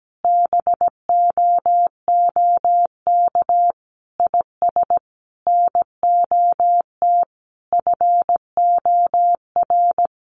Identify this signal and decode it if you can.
Morse code, heard as a keyed tone, about 17 wpm, 700 Hz. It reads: BOOK IS NOT FOR
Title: Sentences from Top 1400 Words 17wpm